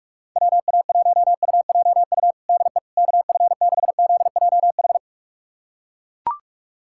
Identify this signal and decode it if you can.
Morse code, heard as a keyed tone, about 41 wpm, 700 Hz. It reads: WA1UJU DE KF6ZJH
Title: USA Callsign DE Callsign 45wpm